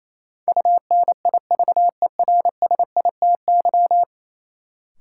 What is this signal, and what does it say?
Morse code, heard as a keyed tone, about 28 wpm, 700 Hz. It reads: UNIVERSITY
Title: Sentences from Top 1000 Words - Review Part 4 - New Word Drill - 28wpm